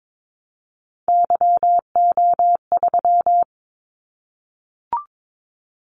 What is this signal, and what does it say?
Morse code, heard as a keyed tone, about 22 wpm, 700 Hz. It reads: YO3
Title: Callsign Prefixes for Top 25 DXCC - 22wpm